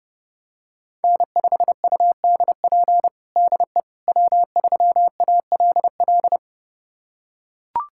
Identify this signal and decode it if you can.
Morse code, heard as a keyed tone, about 30 wpm, 700 Hz. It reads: N5UDP DE W3ALL
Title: USA Callsign DE Callsign - 5 Characters - 30wpm